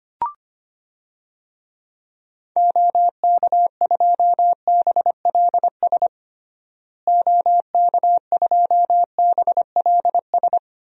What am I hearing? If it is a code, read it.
Morse code, heard as a keyed tone, about 25 wpm, 700 Hz. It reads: OK2BLS OK2BLS
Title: DXCC-100 Call Signs - 6 Characters - Repeated 2x - 25wpm